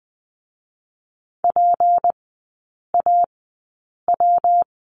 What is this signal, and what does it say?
Morse code, heard as a keyed tone, about 20 wpm, spaced as for 8 wpm, 700 Hz. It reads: PAW
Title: 152 Lesson 25e - Letter p - Three Characters 20wpm